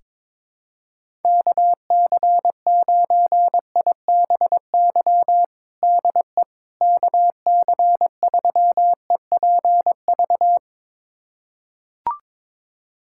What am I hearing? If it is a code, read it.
Morse code, heard as a keyed tone, about 22 wpm, 700 Hz. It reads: KC9IBY DE KC3EPV
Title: USA Callsign DE Callsign - 6 Characters - 22wpm